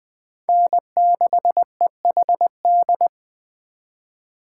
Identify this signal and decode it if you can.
Morse code, heard as a keyed tone, about 24 wpm, 700 Hz. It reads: N6EHD
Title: Speed Racing - USA Call Signs 20wpm